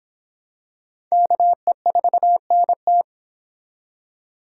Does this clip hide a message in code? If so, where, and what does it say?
Yes: Morse code from 1.1 to 3.0 s, heard as a keyed tone, about 24 wpm, 700 Hz. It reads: KE4NT